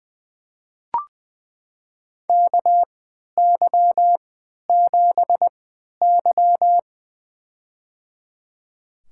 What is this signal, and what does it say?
Morse code, heard as a keyed tone, about 20 wpm, spaced as for 11 wpm, 700 Hz. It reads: KY7Y
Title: USA Call Signs - 4 Characters - 3x Character Spacing - 20wpm